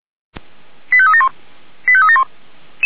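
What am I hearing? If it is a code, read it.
Five-tone selective call (ZVEI), heard as quick runs of five tones, tones of about 70 ms. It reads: 85371 85371
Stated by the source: ZVEI five-tone call received by SDR.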